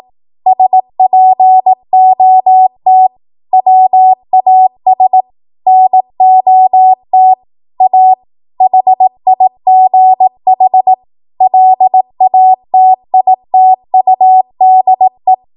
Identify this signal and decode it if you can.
Morse code, heard as a keyed tone, about 18 wpm, 750 Hz. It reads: SPOT WAS NOT A HIGH LATITUDE
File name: Archive / 18 WPM / 18WPM